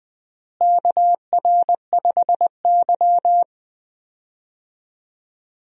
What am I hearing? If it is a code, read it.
Morse code, heard as a keyed tone, about 20 wpm, 700 Hz. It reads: KR5Y
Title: USA Call Signs - 4 Characters - Repeated 2x - 20wpm